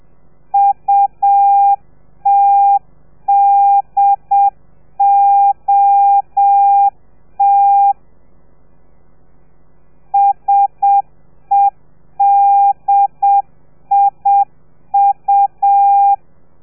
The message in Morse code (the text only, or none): UTDOT SEDIU